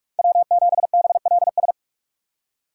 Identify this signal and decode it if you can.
Morse code, heard as a keyed tone, about 44 wpm, 700 Hz. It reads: W7BLS
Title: USA Call Signs - 5 Characters - 4x Character Spacing - 45wpm